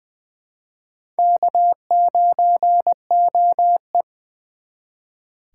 Morse code, heard as a keyed tone, about 23 wpm, 700 Hz. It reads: K9OE
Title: Speed Racing - USA 1x2 Call Signs 20wpm